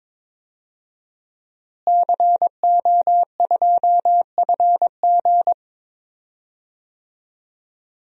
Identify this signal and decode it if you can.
Morse code, heard as a keyed tone, about 22 wpm, 700 Hz. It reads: CO2FG
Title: DXCC-100 Call Signs - 5 Characters - Repeated 2x - 22wpm